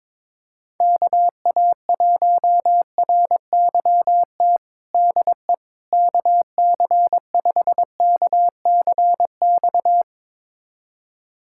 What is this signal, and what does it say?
Morse code, heard as a keyed tone, about 22 wpm, 700 Hz. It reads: KA1RYT DE KC5KCX
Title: USA Callsign DE Callsign - 6 Characters - 22wpm